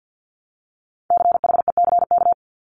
Morse code, heard as a keyed tone, about 50 wpm, 700 Hz. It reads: CHECK
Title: Top 500 Words 50wpm